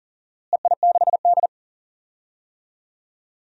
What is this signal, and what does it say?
Morse code, heard as a keyed tone, about 40 wpm, 690 Hz. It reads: EI6D
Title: DXCC-100 Call Signs - 4 Characters - Repeated 2x - 40wpm